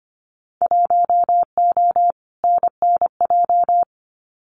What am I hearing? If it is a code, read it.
Morse code, heard as a keyed tone, about 25 wpm, 700 Hz. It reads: 1O NNJ
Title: Winter Field Day Exchanges 25wpm